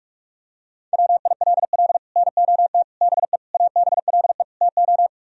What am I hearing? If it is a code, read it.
Morse code, heard as a keyed tone, about 35 wpm, 700 Hz. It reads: WILL NOT BE ABLE TO
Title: Speed-Racing - Sets of 5 Words 30wpm